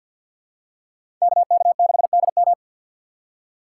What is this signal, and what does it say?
Morse code, heard as a keyed tone, about 52 wpm, 700 Hz. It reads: KK6DM
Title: USA Call Signs - 5 Characters - 6x Character Spacing - 50wpm